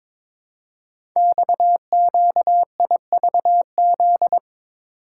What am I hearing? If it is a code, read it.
Morse code, heard as a keyed tone, about 22 wpm, 700 Hz. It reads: XQIVZ
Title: Code Groups - 5 Letters - 22wpm